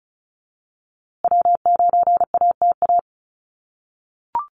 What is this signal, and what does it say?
Morse code, heard as a keyed tone, about 34 wpm, 700 Hz. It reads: W9ATA
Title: USA Call Signs Repeated 3x 35wpm